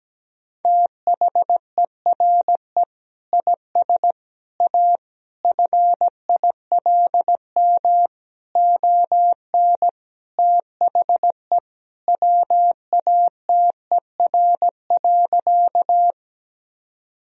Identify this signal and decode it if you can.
Morse code, heard as a keyed tone, about 17 wpm, 700 Hz. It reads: THERE IS A FILM ON THE WATER.
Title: Sentences from Top 1000 Words - Review Part 3 - 17wpm